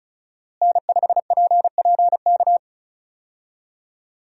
Morse code, heard as a keyed tone, about 36 wpm, 700 Hz. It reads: N5PPK
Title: USA Call Signs - 5 Characters - Repeated 3x - 35wpm